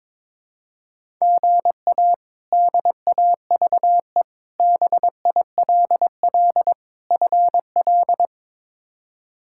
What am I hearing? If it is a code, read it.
Morse code, heard as a keyed tone, about 22 wpm, 700 Hz. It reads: GA DAVE BILL FL
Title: K1USN SST QSO Practice - 22wpm